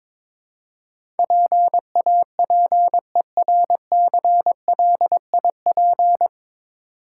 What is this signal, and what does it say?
Morse code, heard as a keyed tone, about 22 wpm, 690 Hz. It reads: PAPERCLIP
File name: Easy Words - Speed-Racing with Hints - 22wpm